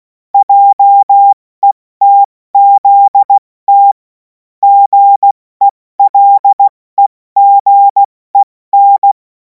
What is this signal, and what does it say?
Morse code, heard as a keyed tone, about 16 wpm, spaced as for 14 wpm, 800 Hz. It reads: JETZT GELEGEN
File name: Morsekurs / Witze / Witz 28. April 2026 70 F80